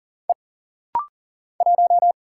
Morse code, heard as a keyed tone, about 40 wpm, 700 Hz. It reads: E 1
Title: Single Letter-Number Alt - Rapid-Fire 40wpm